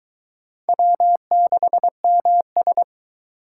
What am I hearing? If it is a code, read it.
Morse code, heard as a keyed tone, about 22 wpm, 700 Hz. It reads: W6MS